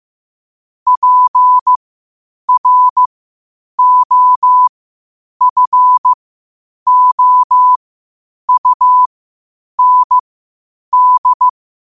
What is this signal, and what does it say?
Morse code, heard as a keyed tone, about 15 wpm, spaced as for 8 wpm, 1000 Hz. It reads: PROFOUND